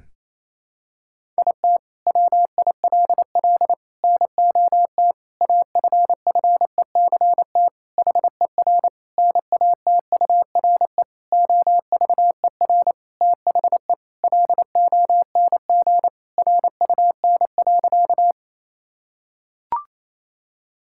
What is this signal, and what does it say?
Morse code, heard as a keyed tone, about 28 wpm, 700 Hz. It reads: IT WILL NOT AFFECT HER NATURE OVER THE LONG RUN.
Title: Sentences from Top 1000 Words - Review Part 6 - 28wpm